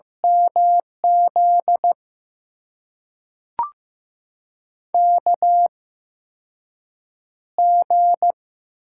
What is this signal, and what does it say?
Morse code, heard as a keyed tone, about 15 wpm, spaced as for 6 wpm, 700 Hz. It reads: MZ KG
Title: USA Call Signs - 6 Characters - 8x Character Spacing - 15wpm